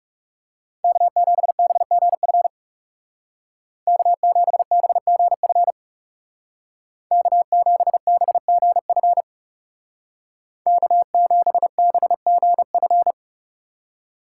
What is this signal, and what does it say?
Morse code, heard as a keyed tone, about 40 wpm, 700 Hz. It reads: K7BGF K7BGF K7BGF K7BGF